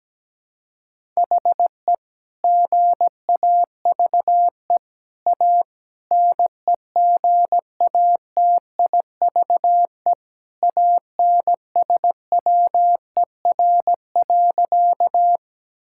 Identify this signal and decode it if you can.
Morse code, heard as a keyed tone, about 17 wpm, 700 Hz. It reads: HE GAVE A NEGATIVE ANSWER.
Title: Sentences from Top 1000 Words - Review Part 6 - 17wpm